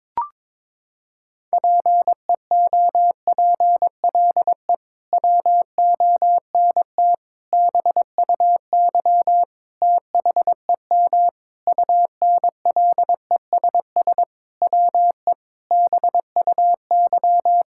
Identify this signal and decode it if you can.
Morse code, heard as a keyed tone, about 22 wpm, 700 Hz. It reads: PEOPLE WONT BUY THEM UNLESS WE BUY
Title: Sentences from Top 1000 Words - Review Part 6 - 22wpm